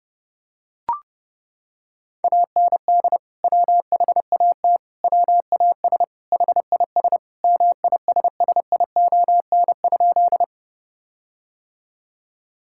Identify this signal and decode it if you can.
Morse code, heard as a keyed tone, about 30 wpm, 700 Hz. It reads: AND WHAT WAS HIS MISSION?
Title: Sentences from Top 1600 Words 30wpm